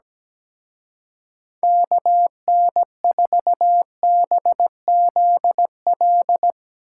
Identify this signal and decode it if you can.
Morse code, heard as a keyed tone, about 18 wpm, 700 Hz. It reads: KN4BZL
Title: Speed Racing - USA 2x3 Call Signs 17wpm